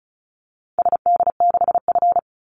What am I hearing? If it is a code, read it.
Morse code, heard as a keyed tone, about 34 wpm, 700 Hz. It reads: SD6F
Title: Call Signs Repeated 2x - 35wpm